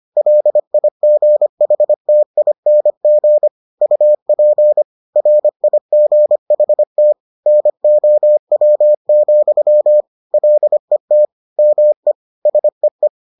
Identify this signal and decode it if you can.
Morse code, heard as a keyed tone, about 25 wpm, 600 Hz. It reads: LIGHTING UP RIGHT NOW, LET ME SEE